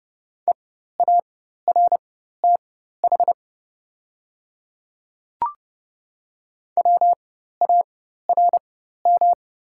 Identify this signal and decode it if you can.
Morse code, heard as a keyed tone, about 30 wpm, spaced as for 14 wpm, 700 Hz. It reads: EARTH WARM
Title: 4x Character Spacing - Top 500 Words - 30wpm